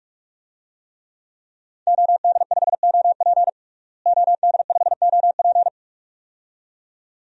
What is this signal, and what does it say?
Morse code, heard as a keyed tone, about 41 wpm, 700 Hz. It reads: OD5OP OD5OP
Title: DXCC-100 Call Signs - 5 Characters - Repeated 2x - 45wpm